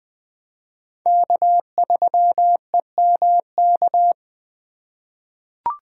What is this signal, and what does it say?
Morse code, heard as a keyed tone, about 20 wpm, 700 Hz. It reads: K3EMK
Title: USA Call Signs - 5 Characters - 4x Character Spacing - 20wpm